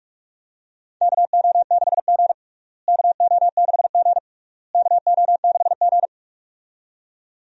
Morse code, heard as a keyed tone, about 42 wpm, 700 Hz. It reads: KO6G KO6G KO6G
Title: USA Call Signs - 4 Characters - Repeated 3x - 45wpm